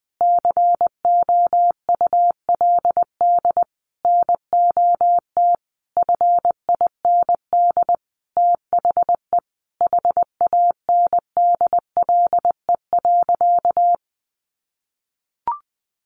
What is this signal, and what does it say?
Morse code, heard as a keyed tone, about 20 wpm, 700 Hz. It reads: COULD NOT FIND THE HANDLE.
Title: Sentences from Top 1100 Words 20wpm